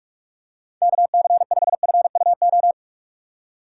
Morse code, heard as a keyed tone, about 46 wpm, 700 Hz. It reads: KC5FUO
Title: USA Call Signs - 6 Characters - 4x Character Spacing - 45wpm